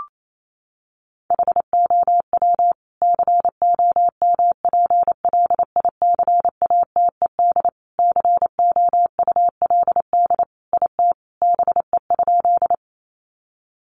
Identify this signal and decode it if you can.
Morse code, heard as a keyed tone, about 28 wpm, 700 Hz. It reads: HOW COMPLICATED COULD IT BE?
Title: Sentences from Top 1200 Words 28wpm